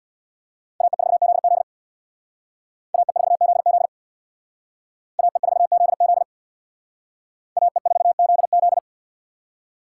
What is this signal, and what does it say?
Morse code, heard as a keyed tone, about 62 wpm, 700 Hz. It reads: AE4ZZ AE4ZZ AE4ZZ AE4ZZ
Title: Speed Racing - USA 2x2 Call Signs 50wpm